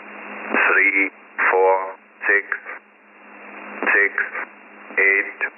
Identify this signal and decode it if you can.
Emergency Alert System (SAME) tones, heard as harsh buzzing bursts, with speech. It is heard from 0.7 to 4.5 s.